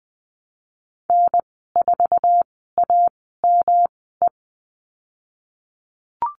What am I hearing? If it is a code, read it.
Morse code, heard as a keyed tone, about 20 wpm, spaced as for 14 wpm, 700 Hz. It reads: N4AME
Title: 154 Lesson 25g - Letter p - Call Signs 20wpm